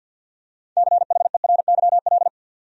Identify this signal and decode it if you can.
Morse code, heard as a keyed tone, about 52 wpm, 700 Hz. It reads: CHERYL
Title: USA Common Names - 50wpm